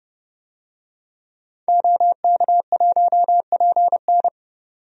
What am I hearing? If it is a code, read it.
Morse code, heard as a keyed tone, about 30 wpm, 700 Hz. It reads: OK1PN
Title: DXCC-100 Call Signs - 5 Characters - 30wpm